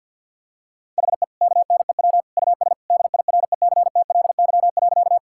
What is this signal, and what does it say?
Morse code, heard as a keyed tone, about 50 wpm, 700 Hz. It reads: HE KNEW US DIRECTLY.
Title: Sentences from Top 1000 Words - Review Part 5 - 50wpm